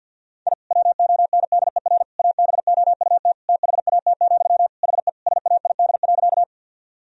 Morse code, heard as a keyed tone, about 46 wpm, 700 Hz. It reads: I WONDER ABOUT THAT, HE SAID.
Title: Sentences from Top 1000 Words - Review Part 3 - 50wpm